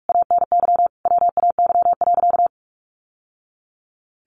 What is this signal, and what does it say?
Morse code, heard as a keyed tone, about 44 wpm, 700 Hz. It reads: ANY WAY.
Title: Sentences from Top 600 Words 45wpm